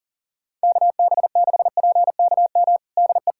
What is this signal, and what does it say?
Morse code, heard as a keyed tone, about 40 wpm, 700 Hz. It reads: KB6PKM DE
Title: Practice QSO - Exchange 4 - Ending - 40wpm